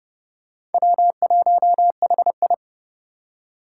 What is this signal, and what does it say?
Morse code, heard as a keyed tone, about 30 wpm, 700 Hz. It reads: W1HI
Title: Four Challenging Letter-Number Combinations 30wpm